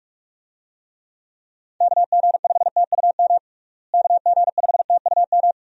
Morse code, heard as a keyed tone, about 46 wpm, 700 Hz. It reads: KG5TUM KG5TUM
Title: USA Call Signs - 6 Characters - Repeated 3x - 45wpm